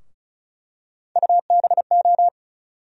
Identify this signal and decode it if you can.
Morse code, heard as a keyed tone, about 36 wpm, 700 Hz. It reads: UBO
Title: USA Call Sign Postfixes - 3 Letters 35wpm